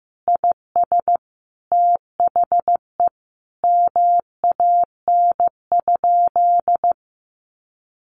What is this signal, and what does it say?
Morse code, heard as a keyed tone, about 15 wpm, 700 Hz. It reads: IS THE MAN?
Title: Sentences from Top 200 Words 15wpm